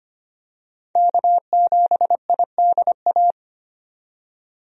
Morse code, heard as a keyed tone, about 25 wpm, 700 Hz. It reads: K7IDA
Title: USA Call Signs - 5 Characters - 25wpm